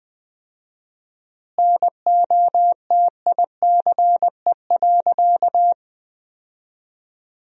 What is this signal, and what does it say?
Morse code, heard as a keyed tone, about 21 wpm, 700 Hz. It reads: NOTICE.
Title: 4x Word-Spacing - Sentences from Top 400 Words - 20wpm